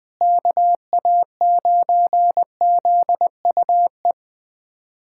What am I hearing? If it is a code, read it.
Morse code, heard as a keyed tone, about 20 wpm, 700 Hz. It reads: KA9ZUE
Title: USA Callsign DE Callsign 20wpm